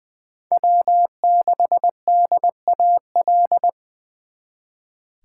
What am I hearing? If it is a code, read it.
Morse code, heard as a keyed tone, about 24 wpm, 700 Hz. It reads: W6DAL
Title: Speed Racing - USA 1x3 Call Signs 20wpm